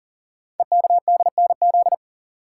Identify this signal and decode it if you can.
Morse code, heard as a keyed tone, about 42 wpm, 700 Hz. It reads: EKDNZ
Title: Code Groups - 5 Letters - 40wpm